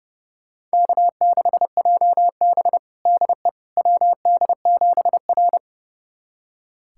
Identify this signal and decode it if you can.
Morse code, heard as a keyed tone, about 29 wpm, 700 Hz. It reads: K6JB DE WD7R
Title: USA Callsign DE Callsign - 4 Characters - 30wpm